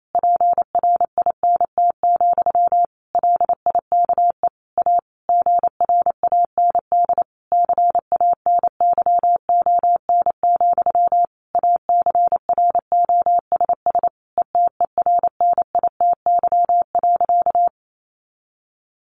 Morse code, heard as a keyed tone, about 28 wpm, 700 Hz. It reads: PRINT, LIKE A GRAND CANYON, ACROSS ETERNITY.
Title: A Sound Of Thunder 28wpm